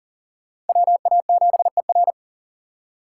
Morse code, heard as a keyed tone, about 50 wpm, 700 Hz. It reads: WA7ER
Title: Speed Racing - USA Call Signs 40wpm